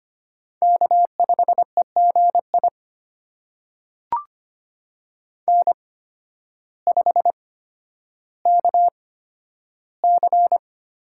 Morse code, heard as a keyed tone, about 25 wpm, spaced as for 12 wpm, 700 Hz. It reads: K5EGI N5KC
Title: USA Call Signs - 5 Characters - 8x Character Spacing - 25wpm